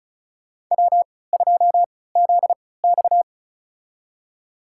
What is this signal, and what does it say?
Morse code, heard as a keyed tone, about 36 wpm, spaced as for 20 wpm, 700 Hz. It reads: W2ZX
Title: USA Call Signs - 4 Characters - 3x Character Spacing - 35wpm